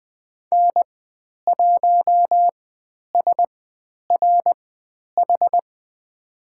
Morse code, heard as a keyed tone, about 20 wpm, spaced as for 10 wpm, 700 Hz. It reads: N1SRH
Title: 102 Lesson 17g - Letter u - Call Signs 20wpm